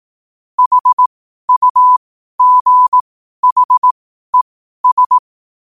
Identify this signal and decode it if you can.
Morse code, heard as a keyed tone, about 18 wpm, spaced as for 12 wpm, 1000 Hz. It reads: HUGHES
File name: qotdfiles / 12wpm